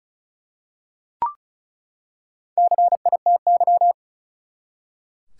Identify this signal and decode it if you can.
Morse code, heard as a keyed tone, about 36 wpm, 700 Hz. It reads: CITY
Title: Four Letter Words - 35wpm